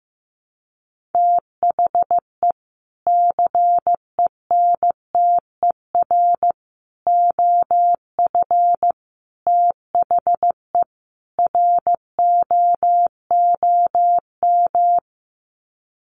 Morse code, heard as a keyed tone, about 15 wpm, 700 Hz. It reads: THE CENTER OF THE ROOM
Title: Sets of 5 Words Encore 15wpm